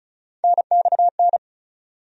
Morse code, heard as a keyed tone, about 36 wpm, 700 Hz. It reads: NXN
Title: USA Call Sign Postfixes - 3 Letters 35wpm